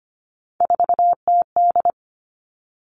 Morse code, heard as a keyed tone, about 25 wpm, 700 Hz. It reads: 4TD
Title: Three Challenging Letter-Number Combinations 25wpm